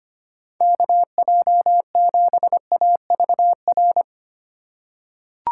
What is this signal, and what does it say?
Morse code, heard as a keyed tone, about 25 wpm, 700 Hz. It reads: KJ7AVR
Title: USA Call Signs - 6 Characters - 3x Character Spacing - 25wpm